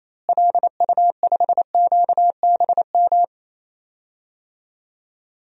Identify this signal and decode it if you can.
Morse code, heard as a keyed tone, about 28 wpm, 700 Hz. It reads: LU5QBM
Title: DXCC-100 Call Signs - 6 Characters - Repeated 2x - 28wpm